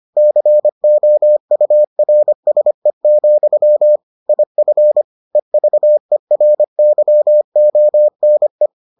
Morse code, heard as a keyed tone, about 25 wpm, 600 Hz. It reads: COURSE, IF EVERYONE